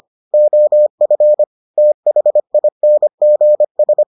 Morse code, heard as a keyed tone, about 25 wpm, 600 Hz. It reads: OF THINGS